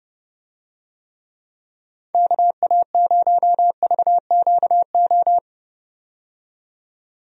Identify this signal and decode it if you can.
Morse code, heard as a keyed tone, about 31 wpm, 700 Hz. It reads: KA0VQO
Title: USA Call Signs - 6 Characters - 30wpm